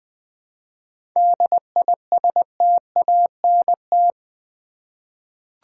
Morse code, heard as a keyed tone, about 21 wpm, 700 Hz. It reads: DISTANT
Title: 7 Letter Words - First 3 Letters Given - 20wpm